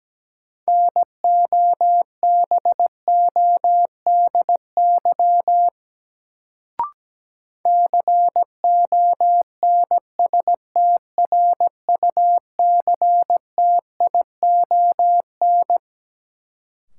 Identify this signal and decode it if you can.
Morse code, heard as a keyed tone, about 17 wpm, 700 Hz. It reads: NOBODY CONSTRUCTION
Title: Sentences from Top 1600 Words - New Word Drill - 17wpm